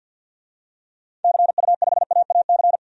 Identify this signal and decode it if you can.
Morse code, heard as a keyed tone, about 52 wpm, 700 Hz. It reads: CU5AAC
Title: DXCC-100 Call Signs - 6 Characters - 50wpm